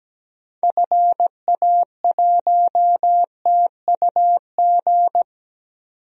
Morse code, heard as a keyed tone, about 17 wpm, 700 Hz. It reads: FA1TUG
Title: DXCC-100 Call Signs - 6 Characters - 17wpm